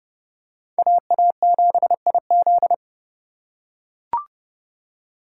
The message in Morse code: AA7IZ

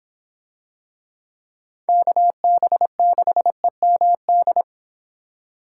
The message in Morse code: KB6EMD

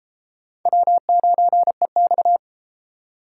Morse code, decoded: W9EX

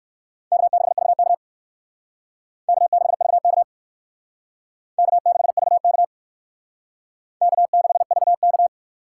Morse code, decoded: K6VK K6VK K6VK K6VK

